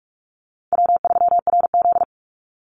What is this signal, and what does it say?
Morse code, heard as a keyed tone, about 44 wpm, 700 Hz. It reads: W3RZ
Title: Easy Call Signs Repeated 3x - 45wpm